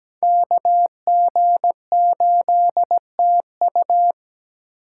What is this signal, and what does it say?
Morse code, heard as a keyed tone, about 20 wpm, 700 Hz. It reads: KG8TU
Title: Speed Racing - USA 2x2 Call Signs 17wpm